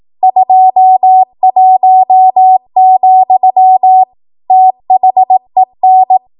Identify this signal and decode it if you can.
Morse code, heard as a keyed tone, about 18 wpm, 750 Hz. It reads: 21, THEN
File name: Archive / 18 WPM / 18WPM